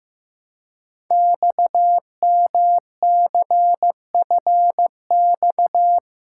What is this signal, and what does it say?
Morse code, heard as a keyed tone, about 15 wpm, 700 Hz. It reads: XMCFX
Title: Code Groups - 5 Letters - 15wpm